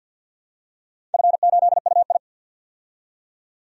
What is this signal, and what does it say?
Morse code, heard as a keyed tone, about 46 wpm, 700 Hz. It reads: F8UI
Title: DXCC-100 Call Signs - 4 Characters - 50wpm